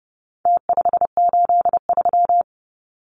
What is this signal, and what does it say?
Morse code, heard as a keyed tone, about 30 wpm, 700 Hz. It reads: T583
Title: Numbers - Contest Style - 4-Digit 30wpm